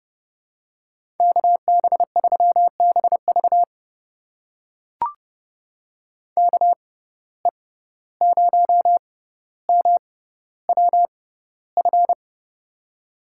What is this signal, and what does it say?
Morse code, heard as a keyed tone, about 30 wpm, spaced as for 15 wpm, 700 Hz. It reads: KB3BV KE0MWF
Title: USA Call Signs - 6x Character Spacing - 30wpm